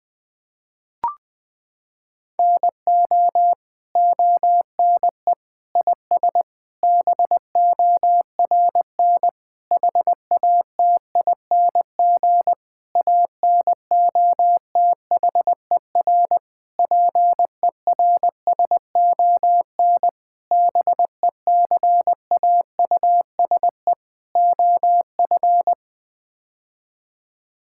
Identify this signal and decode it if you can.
Morse code, heard as a keyed tone, about 20 wpm, 700 Hz. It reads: NO ONE IS BORN HATING ANOTHER PERSON BECAUSE OF
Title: Sets of 9 Words - 20wpm